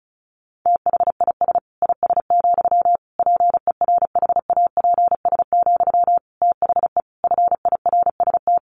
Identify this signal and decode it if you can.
Morse code, heard as a keyed tone, about 34 wpm, 700 Hz. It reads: THIS IS, PERHAPS, THE FIRST
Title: Sentences from Top 1500 Words 35wpm